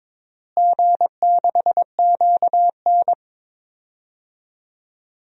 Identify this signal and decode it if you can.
Morse code, heard as a keyed tone, about 22 wpm, 700 Hz. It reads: G6QN
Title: DXCC-100 Call Signs - 4 Characters - Repeated 3x - 22wpm